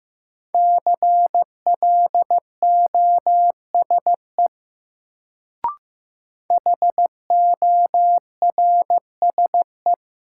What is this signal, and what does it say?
Morse code, heard as a keyed tone, about 15 wpm, 700 Hz. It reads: CLOSE HORSE
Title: Sentences from Top 300 Words - New Word Drill - 15wpm